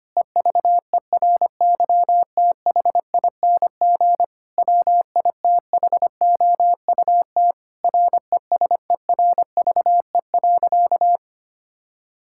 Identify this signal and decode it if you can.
Morse code, heard as a keyed tone, about 25 wpm, 700 Hz. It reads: EVERYTHING WITHOUT RESERVE.
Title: Sentences from Top 1000 Words - Review Part 6 - 25wpm